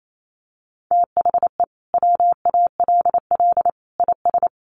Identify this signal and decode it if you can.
Morse code, heard as a keyed tone, about 28 wpm, 700 Hz. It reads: THE WALL IS
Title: Sentences from Top 1400 Words 28wpm